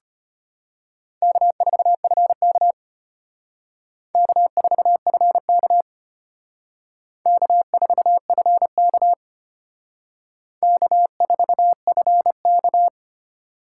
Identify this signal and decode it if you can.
Morse code, heard as a keyed tone, about 29 wpm, 700 Hz. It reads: K4FK K4FK K4FK K4FK
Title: Speed Racing - USA 1x2 Call Signs 25wpm